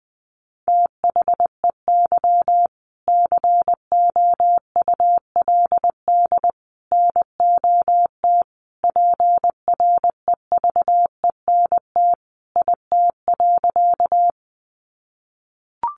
Morse code, heard as a keyed tone, about 20 wpm, 700 Hz. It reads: THEY COULD NOT PREVENT IT.
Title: Sentences from Top 1100 Words 20wpm